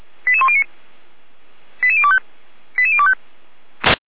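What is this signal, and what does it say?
Five-tone selective call (ZVEI), heard as quick runs of five tones, tones of about 70 ms. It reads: 88109 88025 88025
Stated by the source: ZVEI five-tone selective call 88103 received by SDR, logged by feuerwehr-orlamuende fire brigade server.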